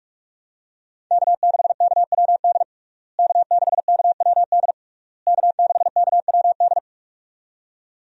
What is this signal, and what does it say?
Morse code, heard as a keyed tone, about 46 wpm, 700 Hz. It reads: K6KWD K6KWD K6KWD
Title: USA Call Signs - 5 Characters - Repeated 3x - 45wpm